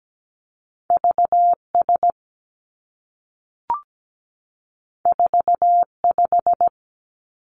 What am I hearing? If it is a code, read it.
Morse code, heard as a keyed tone, about 17 wpm, 700 Hz. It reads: VS 45
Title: Two Challenging Letter-Number Combinations 17wpm